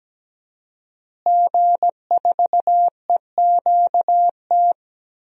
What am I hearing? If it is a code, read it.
Morse code, heard as a keyed tone, about 17 wpm, 700 Hz. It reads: G4EQT
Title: DXCC-100 Call Signs - 5 Characters - Repeated 2x - 17wpm